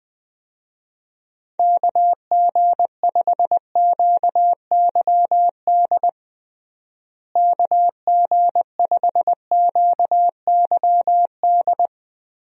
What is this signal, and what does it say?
Morse code, heard as a keyed tone, about 20 wpm, 700 Hz. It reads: KG5QYD KG5QYD
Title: USA Call Signs - 6 Characters - Repeated 3x - 20wpm